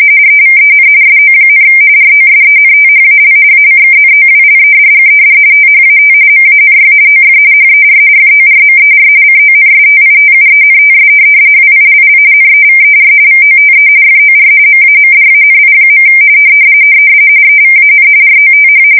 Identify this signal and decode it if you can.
RTTY, about 45.45 baud. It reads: EVEN SO, ROSENTHAL ACKNOWLEDGED THAT MANY OF HER FORMER CONSTITUENTS WILL NEVER VOTE FOR MAMDANI. EVAN ROTH
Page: 170 Shift 2125/2295